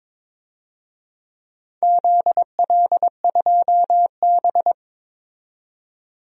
Morse code, heard as a keyed tone, about 22 wpm, 700 Hz. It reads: ZL2B
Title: DXCC-100 Call Signs - 4 Characters - 22wpm